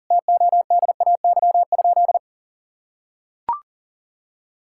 Morse code, heard as a keyed tone, about 40 wpm, 700 Hz. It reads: TODAY?
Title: Sentences from Top 1000 Words - Review Part 5 - 40wpm